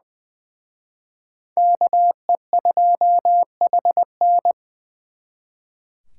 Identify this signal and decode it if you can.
Morse code, heard as a keyed tone, about 24 wpm, 700 Hz. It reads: KE2HN
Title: Speed Racing - USA Call Signs 20wpm